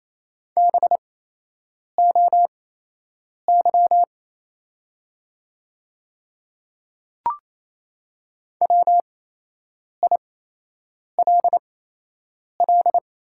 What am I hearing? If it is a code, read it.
Morse code, heard as a keyed tone, about 28 wpm, spaced as for 8 wpm, 700 Hz. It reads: BOY WILL
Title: 8x Character Spacing - Top 200 Words - 28wpm